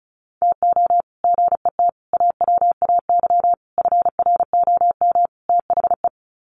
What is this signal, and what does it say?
Morse code, heard as a keyed tone, about 34 wpm, 700 Hz. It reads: TO GET AWAY FROM THE
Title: Sets of 5 Words Encore2 35wpm